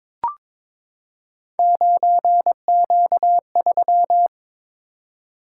Morse code, heard as a keyed tone, about 22 wpm, 700 Hz. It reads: 9Q3
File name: practice / 22wpm / Three Letter-Number Combinations - 22wpm